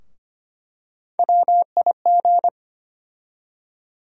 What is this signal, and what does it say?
Morse code, heard as a keyed tone, about 25 wpm, 700 Hz. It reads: WIG
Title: USA Call Sign Postfixes - 3 Letters 25wpm